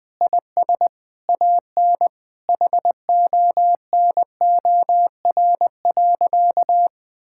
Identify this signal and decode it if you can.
Morse code, heard as a keyed tone, about 20 wpm, 700 Hz. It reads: IS AN HONOR.
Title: Sentences with 6-Word Phrases - 20wpm